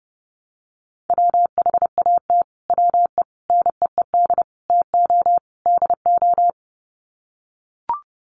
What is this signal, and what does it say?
Morse code, heard as a keyed tone, about 30 wpm, 700 Hz. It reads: WHAT WE NEED TO DO
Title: Sets of 5 Words Encore 30wpm